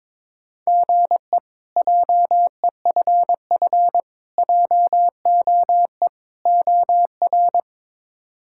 Morse code, heard as a keyed tone, about 21 wpm, 700 Hz. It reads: GE JEFF JOE OR
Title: K1USN SST QSO Practice - 22wpm